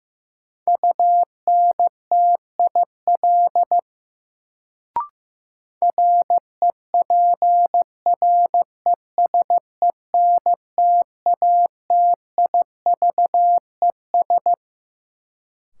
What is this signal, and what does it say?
Morse code, heard as a keyed tone, about 15 wpm, 700 Hz. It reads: UNTIL REPRESENTATIVES
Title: Sets of 5 Words Encore - New Word Drill - 15wpm